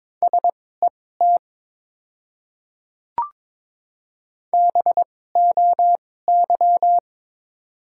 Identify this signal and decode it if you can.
Morse code, heard as a keyed tone, about 22 wpm, spaced as for 16 wpm, 700 Hz. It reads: SET BOY
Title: 2x Character Spacing - Top 200 Words - 22wpm